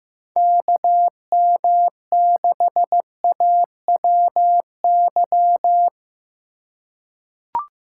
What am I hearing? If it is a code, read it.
Morse code, heard as a keyed tone, about 16 wpm, 700 Hz. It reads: KM6AWY
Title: USA Call Signs - 6 Characters - 4x Character Spacing - 15wpm